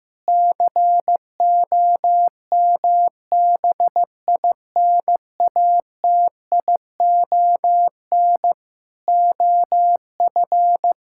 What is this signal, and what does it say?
Morse code, heard as a keyed tone, about 15 wpm, 700 Hz. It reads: COMBINATION OF
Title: Sentences from Top 1000 Words - Review Part 6 - 15wpm